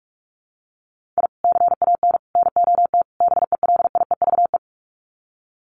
Morse code, heard as a keyed tone, about 44 wpm, 700 Hz. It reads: I CAN NOT BELIEVE
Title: Sets of 4 Words 45wpm